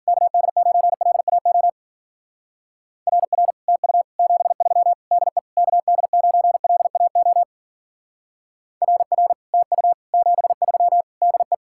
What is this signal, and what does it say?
Morse code, heard as a keyed tone, about 50 wpm, 700 Hz. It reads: KD9LAO RR TU 73 DE KD9LAO RR TU 73 DE
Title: Speed-Racing Common QSO Phrases - 40wpm